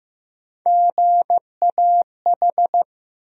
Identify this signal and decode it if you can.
Morse code, heard as a keyed tone, about 15 wpm, 700 Hz. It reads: GAH
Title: USA Call Sign Postfixes - 3 Letters 15wpm